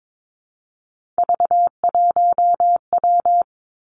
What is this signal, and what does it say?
Morse code, heard as a keyed tone, about 22 wpm, 700 Hz. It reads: V1W
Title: Three Challenging Letter-Number Combinations 22wpm